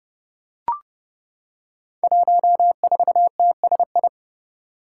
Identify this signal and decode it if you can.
Morse code, heard as a keyed tone, about 32 wpm, 700 Hz. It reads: 14TSI
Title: Five Challenging Letter-Number Combinations 30wpm